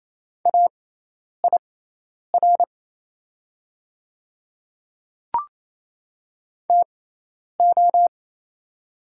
Morse code, heard as a keyed tone, about 28 wpm, spaced as for 10 wpm, 700 Hz. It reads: AIR TO
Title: 6x Character Spacing - Top 200 Words - 28wpm